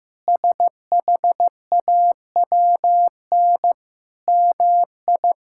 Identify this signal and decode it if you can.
Morse code, heard as a keyed tone, about 15 wpm, 700 Hz. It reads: SHAWN MI
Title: K1USN SST QSO Practice - 15wpm